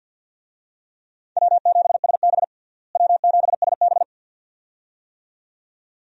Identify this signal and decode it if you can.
Morse code, heard as a keyed tone, about 50 wpm, 700 Hz. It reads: W7SB W7SB
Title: USA Call Signs - 4 Characters - Repeated 2x - 50wpm